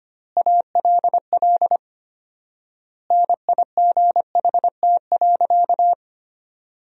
Morse code, heard as a keyed tone, about 25 wpm, 700 Hz. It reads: ALL NIGHT.
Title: 4x Word-Spacing - Sentences from Top 300 Words - 25wpm